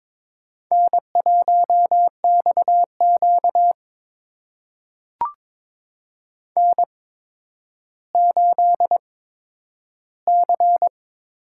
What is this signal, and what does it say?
Morse code, heard as a keyed tone, about 22 wpm, spaced as for 11 wpm, 700 Hz. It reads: N1XQ N8C
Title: USA Call Signs - 4 Characters - 8x Character Spacing - 22wpm